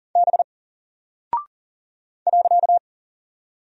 Morse code, heard as a keyed tone, about 38 wpm, 700 Hz. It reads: B .
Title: Single Character (Letter:Number:Punctuation:Prosign) 40wpm